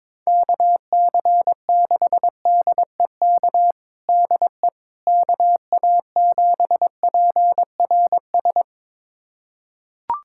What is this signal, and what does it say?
Morse code, heard as a keyed tone, about 22 wpm, 700 Hz. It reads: KC6DEK DE KA7PRS
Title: USA Callsign DE Callsign - 6 Characters - 22wpm